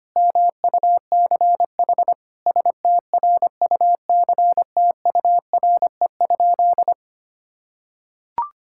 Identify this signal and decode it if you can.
Morse code, heard as a keyed tone, about 25 wpm, 700 Hz. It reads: MUCH STRUCTURE?
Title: Sentences from Top 1000 Words - Review Part 6 - 25wpm